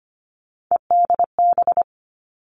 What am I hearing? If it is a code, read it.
Morse code, heard as a keyed tone, about 25 wpm, 700 Hz. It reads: EDB
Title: Three Challenging Letter-Number Combinations 25wpm